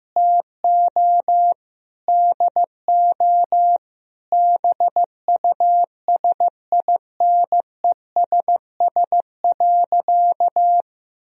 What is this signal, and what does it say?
Morse code, heard as a keyed tone, about 15 wpm, 700 Hz. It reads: TO DO BUSINESS.
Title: Sentences from Top 1000 Words - Review Part 3 - 15wpm